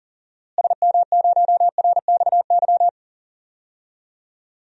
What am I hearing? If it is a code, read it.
Morse code, heard as a keyed tone, about 40 wpm, 700 Hz. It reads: SM0RXY
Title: DXCC-100 Call Signs - 6 Characters - Repeated 3x - 40wpm